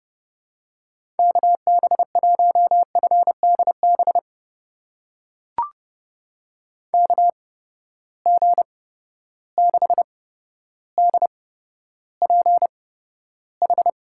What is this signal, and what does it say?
Morse code, heard as a keyed tone, about 29 wpm, spaced as for 13 wpm, 700 Hz. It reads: KB1FDB KG6DPH
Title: USA Call Signs - 6 Characters - 8x Character Spacing - 30wpm